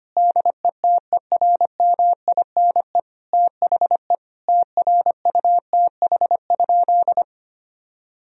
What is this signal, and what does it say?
Morse code, heard as a keyed tone, about 25 wpm, 700 Hz. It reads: DETERMINE THE TRUTH?
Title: Sentences from Top 1000 Words - Review Part 4 - 25wpm